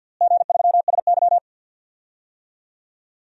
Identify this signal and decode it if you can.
Morse code, heard as a keyed tone, about 53 wpm, 700 Hz. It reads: G3SY
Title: DXCC-100 Call Signs - 4 Characters - Repeated 2x - 50wpm